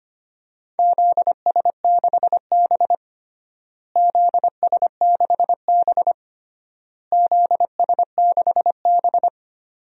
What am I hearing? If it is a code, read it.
Morse code, heard as a keyed tone, about 25 wpm, 700 Hz. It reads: ZS6B ZS6B ZS6B
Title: DXCC-100 Call Signs - 4 Characters - Repeated 3x - 25wpm